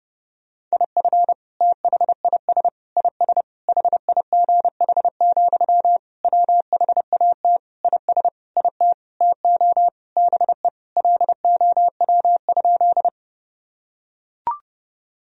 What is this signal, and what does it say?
Morse code, heard as a keyed tone, about 30 wpm, 700 Hz. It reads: IF THIS IS HIGH, WHAT IS IT TO BE LOW?
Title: Sentences from Top 1000 Words - Review Part 1 - 30wpm